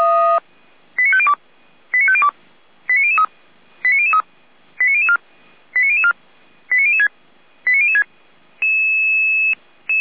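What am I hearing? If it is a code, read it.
Five-tone selective call (ZVEI), heard as quick runs of five tones, tones of about 80 ms. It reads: 89502 89502 89003 89003 89004 89004 89006 89006
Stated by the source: ZVEI five-tone call received on SDR.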